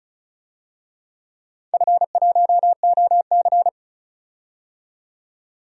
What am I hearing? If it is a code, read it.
Morse code, heard as a keyed tone, about 36 wpm, 700 Hz. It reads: F1OC
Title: DXCC-100 Call Signs - 4 Characters - 35wpm